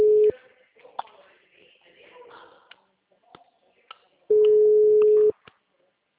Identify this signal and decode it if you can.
Ringback tone (Europe, 425 Hz).